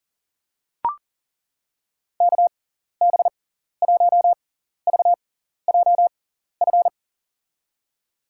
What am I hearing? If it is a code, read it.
Morse code, heard as a keyed tone, about 40 wpm, spaced as for 14 wpm, 700 Hz. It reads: KB1VJF
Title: USA Call Signs - 6 Characters - 6x Character Spacing - 40wpm